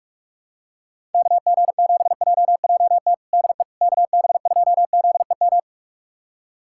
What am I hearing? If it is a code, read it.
Morse code, heard as a keyed tone, about 46 wpm, 700 Hz. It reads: KG7JJT DE KB2ZEM
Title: USA Callsign DE Callsign - 6 Characters - 45wpm